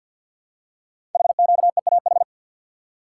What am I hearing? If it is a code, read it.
Morse code, heard as a keyed tone, about 52 wpm, 700 Hz. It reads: HQEAH
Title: Code Groups - 5 Letters - 50wpm